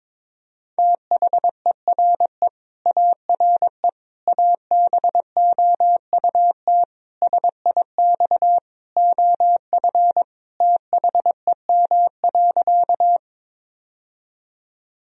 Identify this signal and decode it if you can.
Morse code, heard as a keyed tone, about 22 wpm, 700 Hz. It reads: THERE ARE ABOUT SIX OF THEM.
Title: Sentences from Top 1000 Words - Review Part 2 - 22wpm